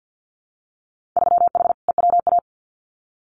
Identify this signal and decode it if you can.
Morse code, heard as a keyed tone, about 50 wpm, 700 Hz. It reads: 3H EWA
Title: Winter Field Day Exchanges Encore 50wpm